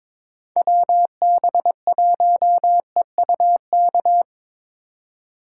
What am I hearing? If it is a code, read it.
Morse code, heard as a keyed tone, about 25 wpm, 700 Hz. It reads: WB1EUK
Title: Speed Racing - USA 2x3 Call Signs 22wpm